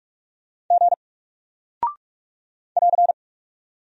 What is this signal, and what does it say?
Morse code, heard as a keyed tone, about 46 wpm, 700 Hz. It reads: G <AR>
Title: Single Character (Letter:Number:Punctuation:Prosign) Alt 45wpm